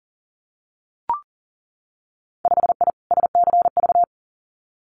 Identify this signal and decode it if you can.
Morse code, heard as a keyed tone, about 38 wpm, 700 Hz. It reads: 5I SCV
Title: Winter Field Day Exchanges Encore 40wpm